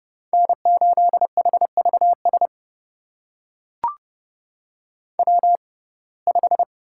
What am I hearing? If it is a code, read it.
Morse code, heard as a keyed tone, about 30 wpm, spaced as for 22 wpm, 700 Hz. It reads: N8HVS W5
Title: USA Call Signs - 5 Characters - 6x Character Spacing - 30wpm